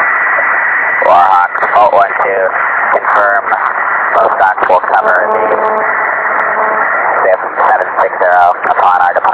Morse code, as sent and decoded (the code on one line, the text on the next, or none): none